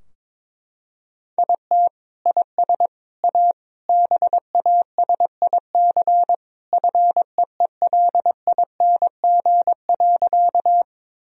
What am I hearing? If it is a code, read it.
Morse code, heard as a keyed tone, about 21 wpm, 700 Hz. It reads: IT IS A BASIC FEELING.
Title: Sentences from Top 1000 Words - Review Part 3 - 22wpm